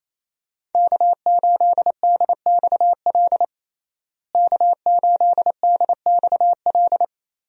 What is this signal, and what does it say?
Morse code, heard as a keyed tone, about 28 wpm, 700 Hz. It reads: K8DXL K8DXL
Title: USA Call Signs - 5 Characters - Repeated 3x - 28wpm